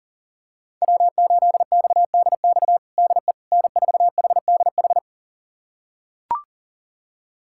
Morse code, heard as a keyed tone, about 40 wpm, 700 Hz. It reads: W8XDX DE N4HDH
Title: USA Callsign DE Callsign - 5 Characters - 40wpm